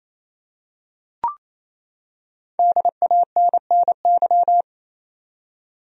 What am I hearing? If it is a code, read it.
Morse code, heard as a keyed tone, about 28 wpm, 700 Hz. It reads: DANNY
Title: USA Common Names - Max 5 Letters - 28wpm